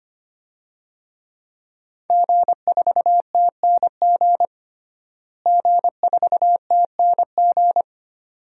Morse code, heard as a keyed tone, about 25 wpm, 700 Hz. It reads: G4TNG G4TNG
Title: DXCC-100 Call Signs - 5 Characters - Repeated 2x - 25wpm